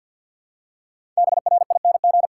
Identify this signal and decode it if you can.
Morse code, heard as a keyed tone, about 53 wpm, 700 Hz. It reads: BRING
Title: Five Letter Words - 50wpm